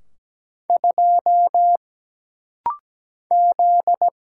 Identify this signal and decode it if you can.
Morse code, heard as a keyed tone, about 17 wpm, 700 Hz. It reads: 2 Z
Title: Single Letter-Number - Rapid-Fire 17wpm